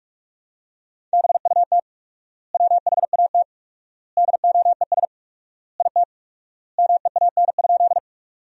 Morse code, heard as a keyed tone, about 43 wpm, 700 Hz. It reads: BUT WHAT DOES IT MEAN?
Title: 4x Word-Spacing - Sentences from Top 200 Words - 45wpm